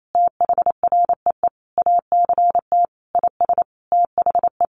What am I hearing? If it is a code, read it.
Morse code, heard as a keyed tone, about 28 wpm, 700 Hz. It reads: THREE ACT IS THE
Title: Sentences from Top 1400 Words 28wpm